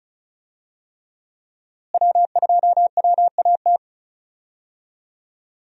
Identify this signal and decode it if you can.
Morse code, heard as a keyed tone, about 36 wpm, 700 Hz. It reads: W2WAT
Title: USA Call Signs - 5 Characters - Repeated 3x - 35wpm